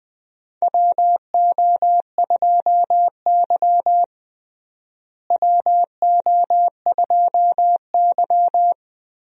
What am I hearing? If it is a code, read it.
Morse code, heard as a keyed tone, about 20 wpm, 700 Hz. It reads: WO2Y WO2Y
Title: USA Call Signs - 4 Characters - Repeated 3x - 20wpm